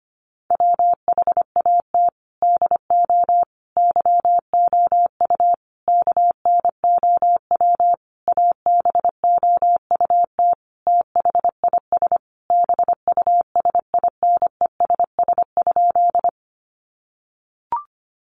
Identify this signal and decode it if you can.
Morse code, heard as a keyed tone, about 25 wpm, 700 Hz. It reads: WHAT DO YOU KNOW ABOUT THIS BUSINESS?
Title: Sentences from Top 600 Words 25wpm